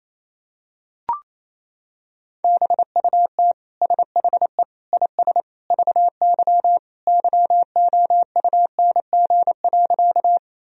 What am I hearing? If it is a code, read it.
Morse code, heard as a keyed tone, about 28 wpm, 700 Hz. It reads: BUT SHE IS VY YOUNG.
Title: Sentences from Top 400 Words and Abbreviations 28wpm